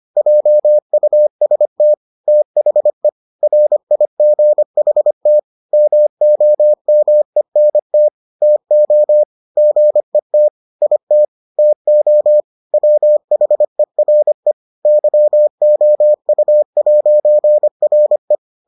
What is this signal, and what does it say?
Morse code, heard as a keyed tone, about 25 wpm, 600 Hz. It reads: JUST THE RIGHT MOMENT TO GET IT TO WHERE YOU'RE